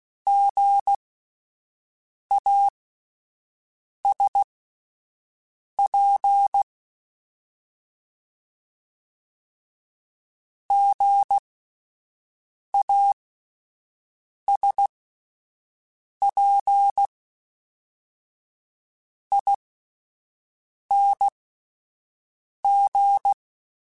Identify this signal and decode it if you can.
Morse code, heard as a keyed tone, about 16 wpm, spaced as for 5 wpm, 800 Hz. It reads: GASP GASPING